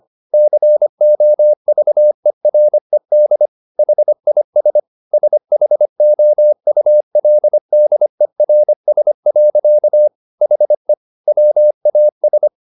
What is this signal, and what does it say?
Morse code, heard as a keyed tone, about 25 wpm, 600 Hz. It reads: COVERED HIS SHOULDERS. HE WAS
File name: the old man and the sea part 1 25wpm